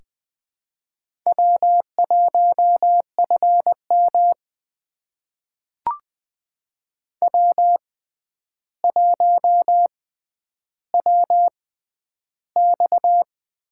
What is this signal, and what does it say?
Morse code, heard as a keyed tone, about 20 wpm, spaced as for 10 wpm, 700 Hz. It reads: W1FM W1WX
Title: USA Call Signs - 6x Character Spacing - 20wpm